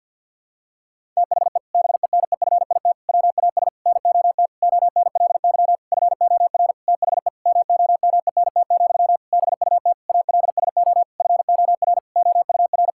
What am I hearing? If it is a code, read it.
Morse code, heard as a keyed tone, about 50 wpm, 700 Hz. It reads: THE BENEFIT WAS NOT ONLY FOR THE MOMENT, BUT ALSO FOR OUR
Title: Sentences from Top 1000 Words - Review Part 6 - 50wpm